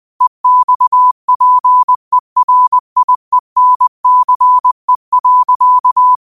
Morse code, heard as a keyed tone, about 20 wpm, 1000 Hz. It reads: EXPERIENCE.